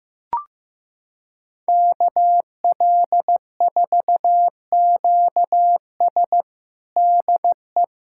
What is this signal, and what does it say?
Morse code, heard as a keyed tone, about 15 wpm, 700 Hz. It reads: KL4QS DE
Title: USA Callsign DE Callsign - 5 Characters - 15wpm